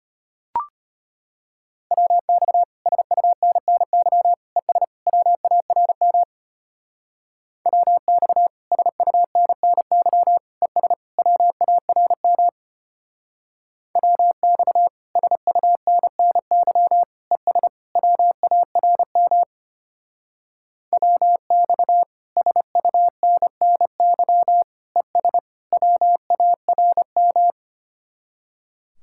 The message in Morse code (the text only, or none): WX SUNNY ES WARM WX SUNNY ES WARM WX SUNNY ES WARM WX SUNNY ES WARM